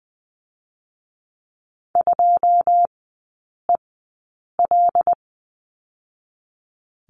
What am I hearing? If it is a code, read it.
Morse code, heard as a keyed tone, about 20 wpm, 700 Hz. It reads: 2 E L
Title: 114 Lesson 19e - Letter w - Three Characters 20wpm